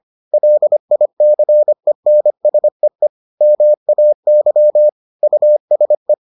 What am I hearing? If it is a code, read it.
Morse code, heard as a keyed tone, about 25 wpm, 600 Hz. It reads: LICENSEE MAY USE